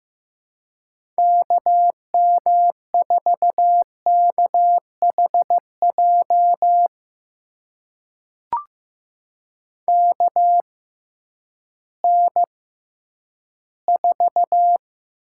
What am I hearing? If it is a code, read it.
Morse code, heard as a keyed tone, about 15 wpm, spaced as for 10 wpm, 700 Hz. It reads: KM4KHJ KN4
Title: USA Call Signs - 6 Characters - 6x Character Spacing - 15wpm